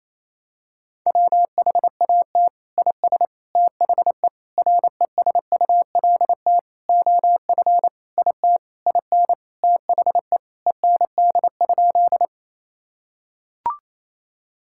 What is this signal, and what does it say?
Morse code, heard as a keyed tone, about 28 wpm, 700 Hz. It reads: WHAT IS THE RESULT OF IT IN THE END?
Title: Sentences from Top 1000 Words - Review Part 5 - 28wpm